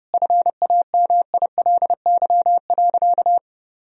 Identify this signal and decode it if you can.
Morse code, heard as a keyed tone, about 30 wpm, 700 Hz. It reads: FAMILY.
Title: Sentences from Top 1000 Words - Review Part 4 - 30wpm